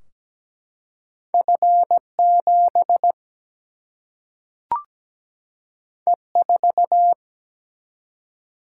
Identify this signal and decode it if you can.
Morse code, heard as a keyed tone, about 17 wpm, 700 Hz. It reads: F 7 E 4
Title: Single-Letter Single-Number 17wpm